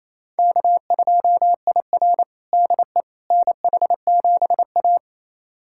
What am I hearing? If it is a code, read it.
Morse code, heard as a keyed tone, about 27 wpm, 700 Hz. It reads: K2IR DE NH7A
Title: USA Callsign DE Callsign - 4 Characters - 28wpm